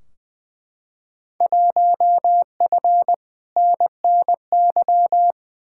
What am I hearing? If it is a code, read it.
Morse code, heard as a keyed tone, about 20 wpm, 700 Hz. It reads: 1F NNY
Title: Field Day Exchanges - 20wpm